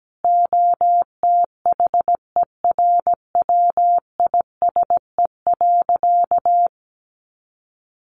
Morse code, heard as a keyed tone, about 17 wpm, 700 Hz. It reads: OTHERWISE.
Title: Sentences from Top 1200 Words 17wpm